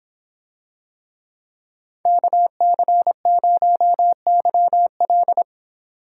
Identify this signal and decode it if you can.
Morse code, heard as a keyed tone, about 26 wpm, 700 Hz. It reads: KC0YL